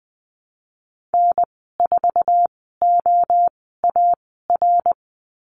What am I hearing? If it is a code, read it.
Morse code, heard as a keyed tone, about 20 wpm, spaced as for 15 wpm, 700 Hz. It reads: N4OAR
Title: 055 Lesson 10g - Letter r - Call Signs 20wpm